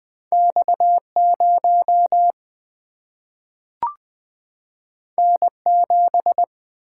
Morse code, heard as a keyed tone, about 20 wpm, 700 Hz. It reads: X 0 N 7
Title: Single-Letter Single-Number 20wpm